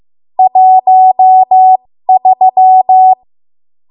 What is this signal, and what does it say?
Morse code, heard as a keyed tone, about 15 wpm, spaced as for 13 wpm, 750 Hz. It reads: 13
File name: Morse / 13WPM